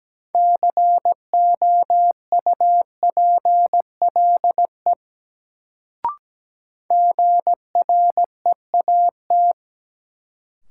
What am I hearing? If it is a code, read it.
Morse code, heard as a keyed tone, about 17 wpm, 700 Hz. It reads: COUPLE GREAT
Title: Sets of 2 Words - New Word Drill - 17wpm